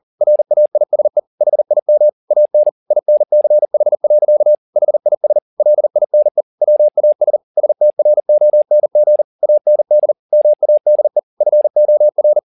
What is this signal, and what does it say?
Morse code, heard as a keyed tone, about 40 wpm, 600 Hz. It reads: RAISE HIM AN INCH. HIS LINE WAS STRONG AND MADE FOR